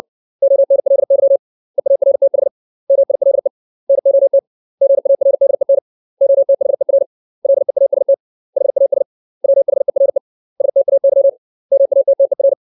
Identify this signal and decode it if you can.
Morse code, heard as a keyed tone, about 60 wpm, 550 Hz. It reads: ONLY EARTH MILE NOT GARDEN OTHER LEAST HAS WHERE STAY MATTER
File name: 632 words / 632 Words 60WPM